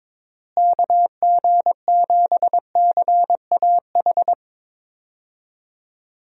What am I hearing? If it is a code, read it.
Morse code, heard as a keyed tone, about 22 wpm, 700 Hz. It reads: KG7CAH
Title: USA Call Signs - 6 Characters - Repeated 2x - 22wpm